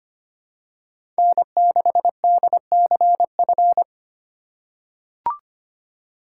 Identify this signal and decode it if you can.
Morse code, heard as a keyed tone, about 25 wpm, 700 Hz. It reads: N6DCF
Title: USA Call Signs - 5 Characters - 8x Character Spacing - 25wpm